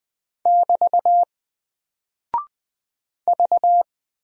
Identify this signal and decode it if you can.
Morse code, heard as a keyed tone, about 20 wpm, 700 Hz. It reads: <BT> V
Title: Single Character (Letter:Number:Punctuation:Prosign) 20wpm